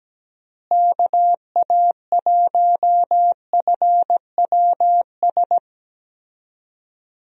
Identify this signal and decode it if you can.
Morse code, heard as a keyed tone, about 17 wpm, 700 Hz. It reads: KA1FWS
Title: USA Call Signs - 6 Characters - Repeated 2x - 17wpm